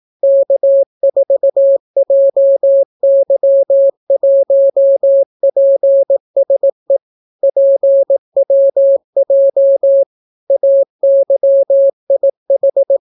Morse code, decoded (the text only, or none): K4JY1PSE PWJ AYIH